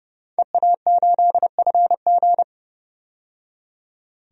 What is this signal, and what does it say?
Morse code, heard as a keyed tone, about 32 wpm, 700 Hz. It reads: EA8FG
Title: DXCC-100 Call Signs - 5 Characters - 30wpm